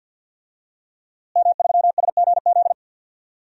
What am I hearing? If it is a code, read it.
Morse code, heard as a keyed tone, about 50 wpm, 700 Hz. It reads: M3SGZ
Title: DXCC-100 Call Signs - 5 Characters - Repeated 2x - 50wpm